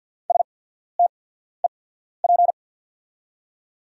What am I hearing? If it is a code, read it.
Morse code, heard as a keyed tone, about 48 wpm, spaced as for 14 wpm, 700 Hz. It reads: STEP
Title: 8x Character Spacing - Top 500 Words - 50wpm